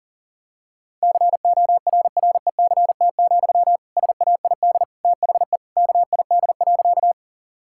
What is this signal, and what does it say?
Morse code, heard as a keyed tone, about 40 wpm, 700 Hz. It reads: CORRECT, SAID THE KID.
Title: Sentences from Top 1600 Words 40wpm